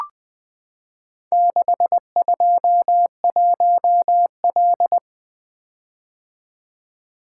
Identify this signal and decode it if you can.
Morse code, heard as a keyed tone, about 20 wpm, 700 Hz. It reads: 621L
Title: Four Letter-Number Combinations - 20wpm